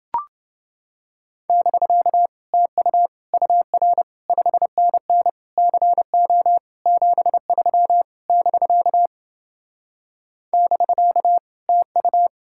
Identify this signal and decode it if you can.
Morse code, heard as a keyed tone, about 29 wpm, 700 Hz. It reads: <BK> TU UR 5NN CO 73 <BK> <BK> TU